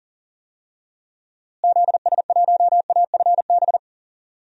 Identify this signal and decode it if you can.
Morse code, heard as a keyed tone, about 38 wpm, 700 Hz. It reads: ZS1AFB
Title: DXCC-100 Call Signs - 6 Characters - Repeated 3x - 40wpm